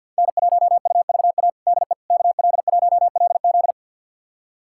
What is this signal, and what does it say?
Morse code, heard as a keyed tone, about 47 wpm, 700 Hz. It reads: N1UVA DE KL1LZ
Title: USA Callsign DE Callsign - 5 Characters - 50wpm